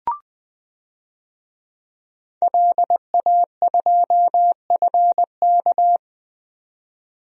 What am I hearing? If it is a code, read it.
Morse code, heard as a keyed tone, about 20 wpm, 700 Hz. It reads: LA2FK
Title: DXCC-100 Call Signs - 5 Characters - 20wpm